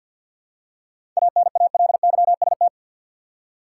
Morse code, heard as a keyed tone, about 52 wpm, 700 Hz. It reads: ANALYST
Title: K1USN SST QSO Practice 50wpm